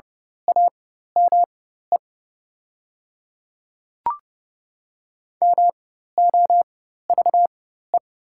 Morse code, heard as a keyed tone, about 30 wpm, spaced as for 14 wpm, 690 Hz. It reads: AME MOVE
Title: 4x Character Spacing - Top 200 Words - 30wpm